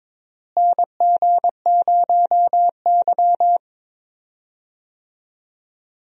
Morse code, heard as a keyed tone, about 21 wpm, 700 Hz. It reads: NG0Y
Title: USA Call Signs - 4 Characters - Repeated 2x - 22wpm